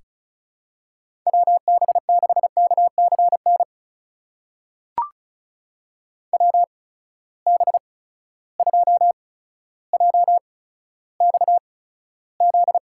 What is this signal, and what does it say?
Morse code, heard as a keyed tone, about 36 wpm, spaced as for 15 wpm, 700 Hz. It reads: WB6KCN WB2JXZ
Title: USA Call Signs - 6 Characters - 8x Character Spacing - 35wpm